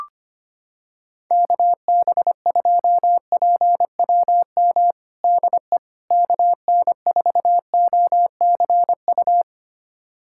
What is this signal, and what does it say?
Morse code, heard as a keyed tone, about 25 wpm, 700 Hz. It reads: KB2PWM DE KN4OCU
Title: USA Callsign DE Callsign - 6 Characters - 25wpm